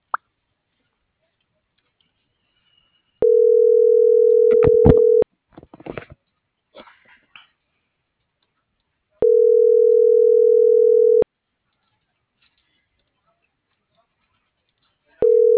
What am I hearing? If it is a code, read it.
Ringback tone (North America), 440 + 480 Hz.